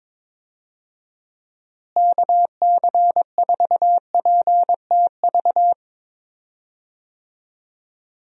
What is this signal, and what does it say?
Morse code, heard as a keyed tone, about 22 wpm, 700 Hz. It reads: KC4PTV
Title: USA Call Signs - 6 Characters - Repeated 2x - 22wpm